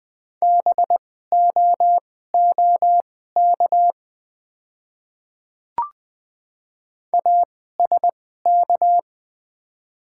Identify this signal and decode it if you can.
Morse code, heard as a keyed tone, about 20 wpm, spaced as for 15 wpm, 690 Hz. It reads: BOOK ASK
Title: 2x Character Spacing - Top 300 Words - 20wpm